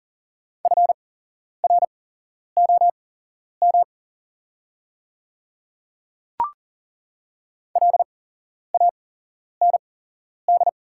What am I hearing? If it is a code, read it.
Morse code, heard as a keyed tone, about 40 wpm, spaced as for 11 wpm, 700 Hz. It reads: FROM LAND
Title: 8x Character Spacing - Top 200 Words - 40wpm